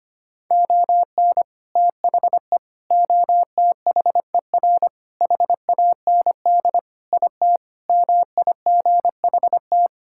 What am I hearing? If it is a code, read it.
Morse code, heard as a keyed tone, about 25 wpm, 700 Hz. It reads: ON THE OTHER HAND IT MIGHT
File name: practice / 25wpm / Sentences with 3-Word Phrases - 25wpm